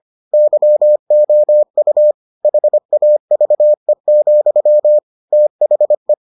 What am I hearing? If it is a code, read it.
Morse code, heard as a keyed tone, about 25 wpm, 600 Hz. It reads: YOU HAVE, THE